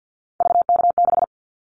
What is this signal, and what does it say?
Morse code, heard as a keyed tone, about 48 wpm, 700 Hz. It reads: VK6
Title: Callsign Prefixes for Top 100 DXCC - 50wpm